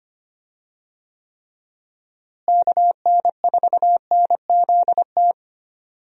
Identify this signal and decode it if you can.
Morse code, heard as a keyed tone, about 25 wpm, 700 Hz. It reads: KN4NZT
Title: USA Call Signs - 6 Characters - Repeated 3x - 25wpm